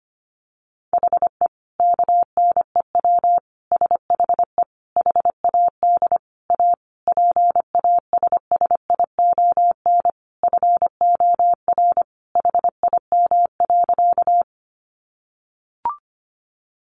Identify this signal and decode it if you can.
Morse code, heard as a keyed tone, about 25 wpm, 700 Hz. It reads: HE KNEW SHE HAD A PASSION FOR HIM.
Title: Sentences from Top 1300 Words 25wpm